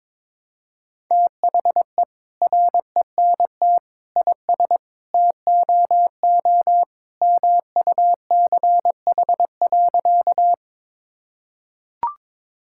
Morse code, heard as a keyed tone, about 21 wpm, 700 Hz. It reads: THE RENT IS TOO MUCH.
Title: Sentences from Top 1000 Words - Review Part 5 - 22wpm